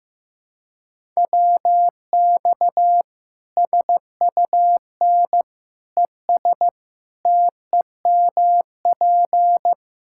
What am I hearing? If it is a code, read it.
Morse code, heard as a keyed tone, about 16 wpm, 700 Hz. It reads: WX SUN ES TEMP
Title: Practice QSO - Exchange 2 - Info - 15wpm